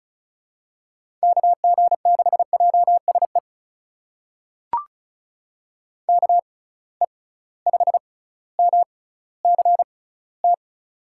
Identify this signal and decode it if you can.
Morse code, heard as a keyed tone, about 36 wpm, spaced as for 18 wpm, 700 Hz. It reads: KG6JSE KE5MCT
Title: USA Call Signs - 6 Characters - 6x Character Spacing - 35wpm